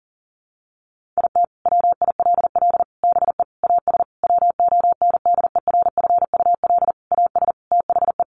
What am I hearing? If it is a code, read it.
Morse code, heard as a keyed tone, about 40 wpm, 700 Hz. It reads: IT WILL BE AS WONDERFUL AS THE
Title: Sentences from Top 1400 Words 40wpm